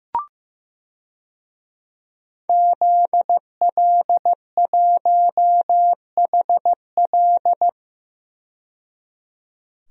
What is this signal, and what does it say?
Morse code, heard as a keyed tone, about 15 wpm, 700 Hz. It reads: ZL1HL
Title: DXCC-100 Call Signs - 5 Characters - 15wpm